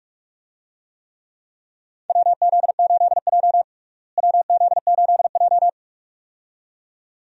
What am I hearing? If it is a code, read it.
Morse code, heard as a keyed tone, about 46 wpm, 700 Hz. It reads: WZ8J WZ8J
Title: USA Call Signs - 4 Characters - Repeated 2x - 45wpm